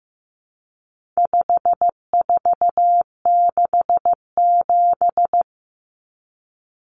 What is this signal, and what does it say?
Morse code, heard as a keyed tone, about 15 wpm, 700 Hz. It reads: 5467
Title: Common Cut Numbers - 4-Digit 15wpm